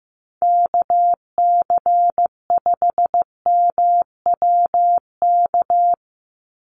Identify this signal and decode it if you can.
Morse code, heard as a keyed tone, about 15 wpm, 700 Hz. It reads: KC5MWK
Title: USA Call Signs 15wpm